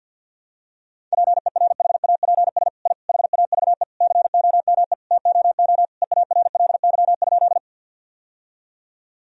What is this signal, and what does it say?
Morse code, heard as a keyed tone, about 50 wpm, 700 Hz. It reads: PERHAPS I HAVE COME TOO EARLY?
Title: Sentences from Top 1000 Words - Review Part 2 - 50wpm